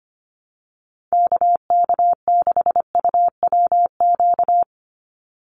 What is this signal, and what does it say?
Morse code, heard as a keyed tone, about 25 wpm, 700 Hz. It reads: KK6UWQ
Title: USA Call Signs 25wpm